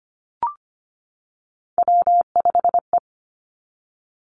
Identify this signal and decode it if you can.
Morse code, heard as a keyed tone, about 25 wpm, 700 Hz. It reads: W5E
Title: Three Challenging Letter-Number Combinations 25wpm